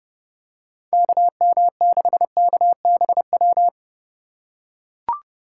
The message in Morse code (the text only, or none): KM6KBW